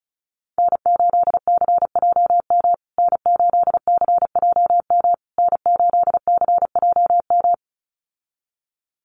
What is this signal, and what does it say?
Morse code, heard as a keyed tone, about 34 wpm, 700 Hz. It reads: N8CJM N8CJM N8CJM
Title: USA Call Signs Repeated 3x 35wpm